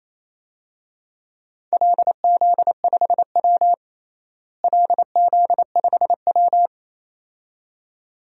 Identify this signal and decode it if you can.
Morse code, heard as a keyed tone, about 28 wpm, 700 Hz. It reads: LZ5W LZ5W
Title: DXCC-100 Call Signs - 4 Characters - Repeated 2x - 28wpm